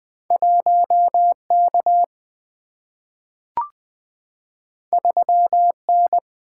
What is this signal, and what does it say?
Morse code, heard as a keyed tone, about 20 wpm, 690 Hz. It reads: 1K 3N
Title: Two Letter-Number Combinations - 20wpm